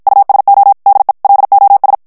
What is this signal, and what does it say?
Morse code, heard as a keyed tone, about 54 wpm, 800 Hz. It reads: USO DE LOS
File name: ESPAGNOL TEXTE EN CLAIR 1 VITESSE 50 MOTS MINUTES